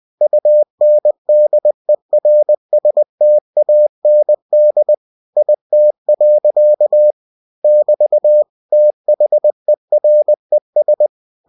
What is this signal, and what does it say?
Morse code, heard as a keyed tone, about 20 wpm, 600 Hz. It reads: UNDERSTAND IT. <BT> THERES